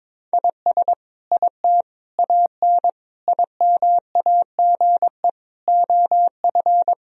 Morse code, heard as a keyed tone, about 22 wpm, 700 Hz. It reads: IS IT AN IMAGE OF
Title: Sentences from Top 1000 Words - Review Part 4 - 22wpm